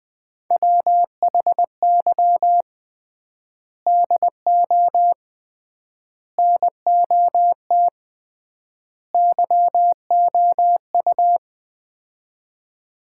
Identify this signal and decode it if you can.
Morse code, heard as a keyed tone, about 20 wpm, 700 Hz. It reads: WHY DO NOT YOU
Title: 3x Word-Spacing - Sets of 4 Words - 20wpm